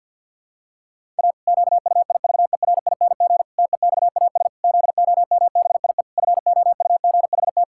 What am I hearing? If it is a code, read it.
Morse code, heard as a keyed tone, about 50 wpm, 700 Hz. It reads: A QUIVERING TEXAS ZOMBIE FOUGHT
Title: Pangram Challenge v2 - 50wpm